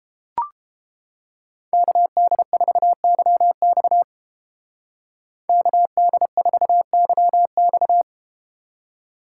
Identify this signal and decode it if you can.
Morse code, heard as a keyed tone, about 32 wpm, 700 Hz. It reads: KD4YX KD4YX